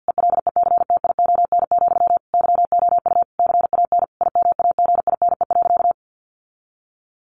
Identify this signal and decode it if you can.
Morse code, heard as a keyed tone, about 50 wpm, 700 Hz. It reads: ELECTION, YOU CAN IMAGINE.
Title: Sentences from Top 1500 Words 50wpm